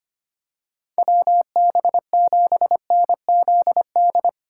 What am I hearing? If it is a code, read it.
Morse code, heard as a keyed tone, about 25 wpm, 700 Hz. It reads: WB7NZD
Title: USA Call Signs - 3x Character Spacing - 25wpm